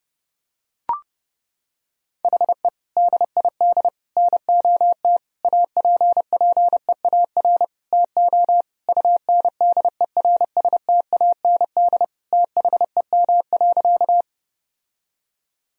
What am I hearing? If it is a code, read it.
Morse code, heard as a keyed tone, about 32 wpm, 700 Hz. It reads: HE DID NOT APPEAR TO UNDERSTAND THEM.
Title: Sentences from Top 1000 Words - Review Part 2 - 30wpm